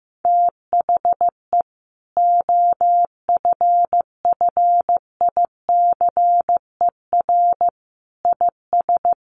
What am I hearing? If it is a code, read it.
Morse code, heard as a keyed tone, about 15 wpm, 700 Hz. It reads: THE OFFICER IS
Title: Sentences from Top 1400 Words 15wpm